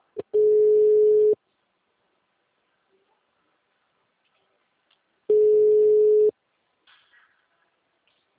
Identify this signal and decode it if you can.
Ringback tone (Europe, 425 Hz).